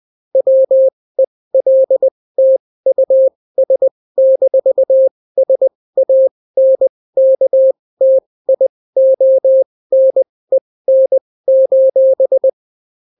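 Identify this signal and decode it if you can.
Morse code, heard as a keyed tone, about 20 wpm, spaced as for 16 wpm, 530 Hz. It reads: WELTUS-SANKTIONEN:
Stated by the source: Daily Morse weather and top stories at 16 wpm, clean without QRN.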